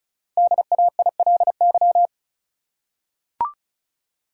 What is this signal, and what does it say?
Morse code, heard as a keyed tone, about 34 wpm, 700 Hz. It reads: DAILY
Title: Five Letter Words - 35wpm